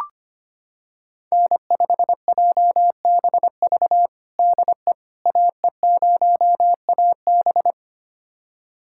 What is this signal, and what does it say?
Morse code, heard as a keyed tone, about 25 wpm, 700 Hz. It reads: N5JBV DE AE0AB
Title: USA Callsign DE Callsign - 5 Characters - 25wpm